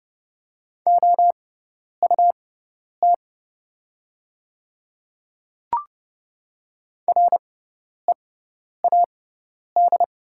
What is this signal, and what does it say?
Morse code, heard as a keyed tone, about 30 wpm, spaced as for 10 wpm, 700 Hz. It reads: OUT READ
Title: 6x Character Spacing - Top 200 Words - 30wpm